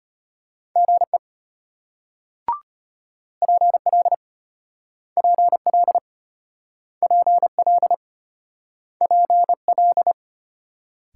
Morse code, heard as a keyed tone, about 33 wpm, 700 Hz. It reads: GE PL PL PL PL